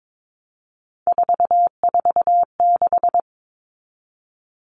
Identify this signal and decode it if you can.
Morse code, heard as a keyed tone, about 22 wpm, 700 Hz. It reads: 446
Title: Numbers - 3-Digit 22wpm